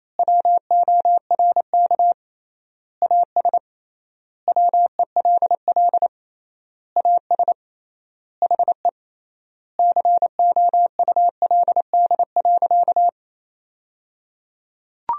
Morse code, heard as a keyed tone, about 28 wpm, 700 Hz. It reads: WORK AS WELL AS HE COULD.
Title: 3x Word-Spacing - Sentences from Top 500 Words - 28wpm